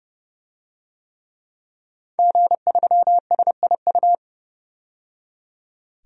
Morse code, heard as a keyed tone, about 30 wpm, 700 Hz. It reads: G3SIU
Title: DXCC-100 Call Signs - 5 Characters - 30wpm